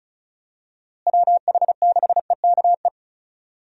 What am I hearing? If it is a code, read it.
Morse code, heard as a keyed tone, about 36 wpm, 700 Hz. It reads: WH6EKE
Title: USA Call Signs - 6 Characters - Repeated 2x - 35wpm